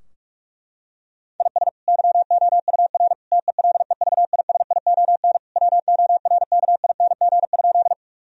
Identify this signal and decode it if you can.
Morse code, heard as a keyed tone, about 46 wpm, 700 Hz. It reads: IS YOUR TELEVISION WORKING?
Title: Sentences from Top 1000 Words - Review Part 4 - 45wpm